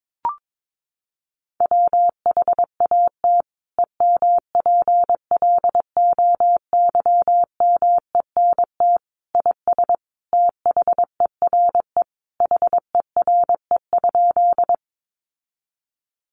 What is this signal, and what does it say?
Morse code, heard as a keyed tone, about 22 wpm, 700 Hz. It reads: WHAT EMPLOYMENT IS THERE HERE?
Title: Sentences from Top 1500 Words 22wpm